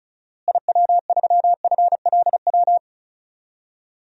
Morse code, heard as a keyed tone, about 34 wpm, 700 Hz. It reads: IW3FLW
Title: DXCC-100 Call Signs - 6 Characters - Repeated 3x - 35wpm